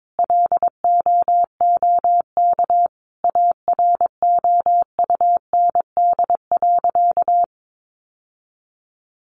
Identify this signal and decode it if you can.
Morse code, heard as a keyed tone, about 22 wpm, 700 Hz. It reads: LOOK AROUND.
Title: Sentences from Top 600 Words 22wpm